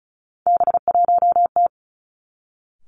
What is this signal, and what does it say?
Morse code, heard as a keyed tone, about 36 wpm, 700 Hz. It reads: B1T
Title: Three Challenging Letter-Number Combinations 35wpm